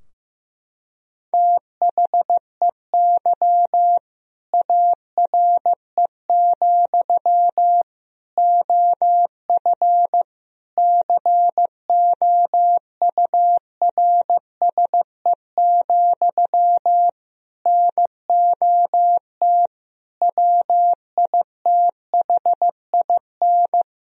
Morse code, heard as a keyed tone, about 15 wpm, 700 Hz. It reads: THEY ARE, OF COURSE, NOT WITHIN
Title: Sentences from Top 1000 Words - Review Part 4 - 15wpm